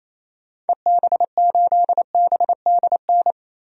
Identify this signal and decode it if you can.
Morse code, heard as a keyed tone, about 28 wpm, 700 Hz. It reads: EB8BDN
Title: DXCC-100 Call Signs - 6 Characters - 28wpm